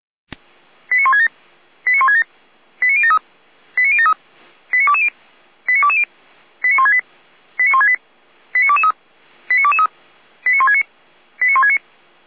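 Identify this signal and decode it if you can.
Five-tone selective call (ZVEI), heard as quick runs of five tones, tones of about 70 ms. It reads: 89167 89167 89073 89073 89220 89220 89168 89168 89223 89223 89169 89169
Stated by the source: ZVEI five-tone selective call, code 89060, received by SDR.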